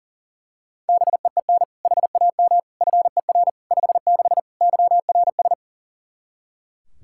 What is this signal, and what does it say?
Morse code, heard as a keyed tone, about 40 wpm, 700 Hz. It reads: BEEN HAM FER 56 YRS
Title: Practice QSO - Exchange 3 - More Info - 40wpm